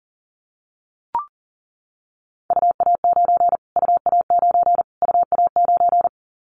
Morse code, heard as a keyed tone, about 40 wpm, 700 Hz. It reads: UA9 UA9 UA9
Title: Callsign Prefixes for Top 25 DXCC - Repeated 3x - 40wpm